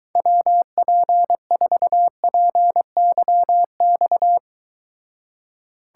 Morse code, heard as a keyed tone, about 23 wpm, 700 Hz. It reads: WP4PYX